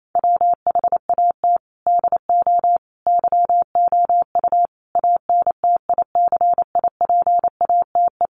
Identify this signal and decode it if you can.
Morse code, heard as a keyed tone, about 28 wpm, 700 Hz. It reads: WHAT DO YOU ANTICIPATE
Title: Sentences from Top 1500 Words 28wpm